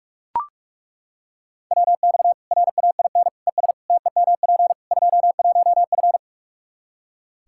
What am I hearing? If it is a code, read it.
Morse code, heard as a keyed tone, about 42 wpm, 700 Hz. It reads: WX RAIN ES TEMP 21F
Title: Practice QSO - Exchange 2 - Info - 45wpm